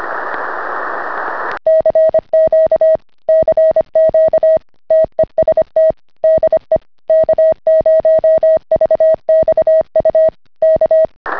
Morse code, heard as a keyed tone, about 25 wpm, 640 Hz. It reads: CQ CQ TEST DE K0VXU K